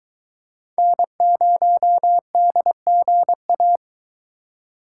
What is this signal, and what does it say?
Morse code, heard as a keyed tone, about 22 wpm, 700 Hz. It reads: N0DGA